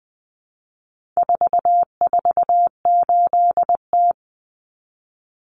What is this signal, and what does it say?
Morse code, heard as a keyed tone, about 20 wpm, 700 Hz. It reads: 448T
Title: Common Cut Numbers - 4-Digit 20wpm